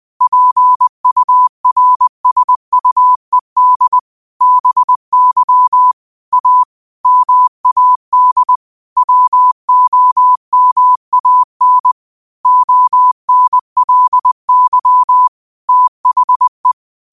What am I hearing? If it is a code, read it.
Morse code, heard as a keyed tone, about 20 wpm, 1000 Hz. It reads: PURSUED BY A MAD WOMAN ONLY THE
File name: qotdfiles / 20wpm